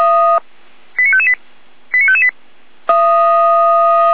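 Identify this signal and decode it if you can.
Five-tone selective call (ZVEI), heard as quick runs of five tones, tones of about 70 ms. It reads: 89558 89558
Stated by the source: ZVEI five-tone call 89123 received by SDR, from feuerwehr-orlamuende.org fire brigade server.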